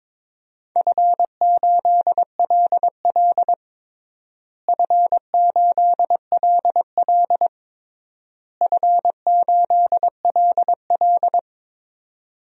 Morse code, heard as a keyed tone, about 22 wpm, 700 Hz. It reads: F8LL F8LL F8LL
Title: DXCC-100 Call Signs - 4 Characters - Repeated 3x - 22wpm